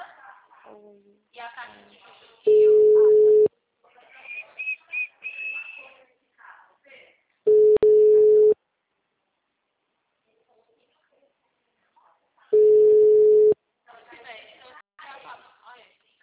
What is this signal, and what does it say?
Ringback tone (Europe, 425 Hz), with speech.
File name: out- -2018- - - .48815